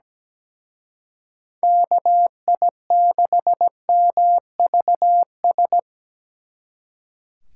Morse code, heard as a keyed tone, about 20 wpm, 700 Hz. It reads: KI6MVS
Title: Speed Racing - USA 2x3 Call Signs 17wpm